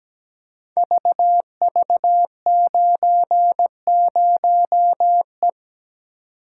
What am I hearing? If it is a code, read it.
Morse code, heard as a keyed tone, about 17 wpm, 700 Hz. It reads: VV90E
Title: Code Groups - 5 Letter:Numbers - 17wpm